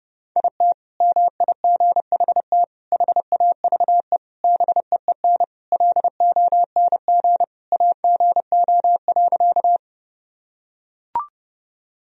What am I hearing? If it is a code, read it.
Morse code, heard as a keyed tone, about 30 wpm, 700 Hz. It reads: IT MIGHT HAVE BEEN LONG AGO.
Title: Sentences from Top 1000 Words - Review Part 3 - 30wpm